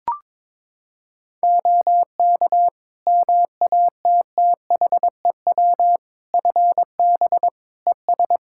Morse code, heard as a keyed tone, about 21 wpm, 700 Hz. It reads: OK MATTHEW FB ES
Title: Practice QSO - Exchange 2 - Info - 22wpm